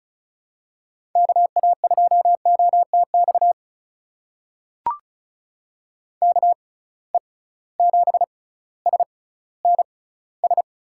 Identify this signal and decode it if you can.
Morse code, heard as a keyed tone, about 36 wpm, spaced as for 18 wpm, 700 Hz. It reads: KA2OTX KE7SNS
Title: USA Call Signs - 6 Characters - 6x Character Spacing - 35wpm